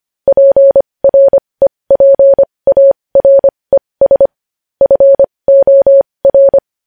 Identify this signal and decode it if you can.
Morse code, heard as a keyed tone, about 25 wpm, spaced as for 20 wpm, 550 Hz. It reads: PREPARES FOR
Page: news-headlines-in-morse-code-at-20-wpm